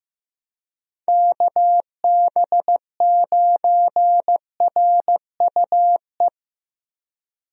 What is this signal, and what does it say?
Morse code, heard as a keyed tone, about 15 wpm, 700 Hz. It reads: KB9RUE
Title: USA Call Signs - 6 Characters - Repeated 3x - 15wpm